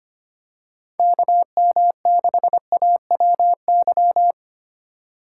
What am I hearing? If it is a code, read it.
Morse code, heard as a keyed tone, about 25 wpm, 700 Hz. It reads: KM6AWY
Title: USA Call Signs - 6 Characters - 3x Character Spacing - 25wpm